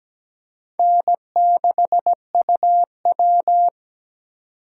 Morse code, heard as a keyed tone, about 17 wpm, 700 Hz. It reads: N6UW
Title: USA Call Signs - 3x Character Spacing - 17wpm